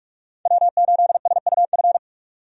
Morse code, heard as a keyed tone, about 47 wpm, 700 Hz. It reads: W8SUF
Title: USA Call Signs - 5 Characters - 45wpm